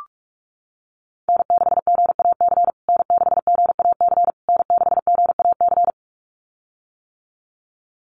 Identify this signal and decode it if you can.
Morse code, heard as a keyed tone, about 44 wpm, 690 Hz. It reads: N6GAC N6GAC N6GAC
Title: USA Call Signs Repeated 3x 45wpm